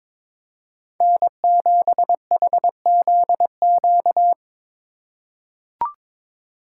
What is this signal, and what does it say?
Morse code, heard as a keyed tone, about 22 wpm, 700 Hz. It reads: N7HZQ
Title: USA Call Signs - 5 Characters - 2x Character Spacing - 22wpm